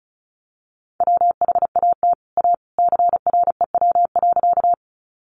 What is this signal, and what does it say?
Morse code, heard as a keyed tone, about 34 wpm, 700 Hz. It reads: WHAT A CREW.
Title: Sentences from Top 1500 Words 35wpm